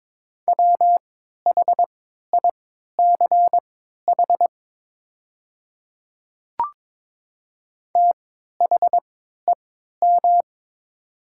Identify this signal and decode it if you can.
Morse code, heard as a keyed tone, about 22 wpm, spaced as for 12 wpm, 700 Hz. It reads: WHICH THEM
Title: 3x Character Spacing - Top 300 Words - 22wpm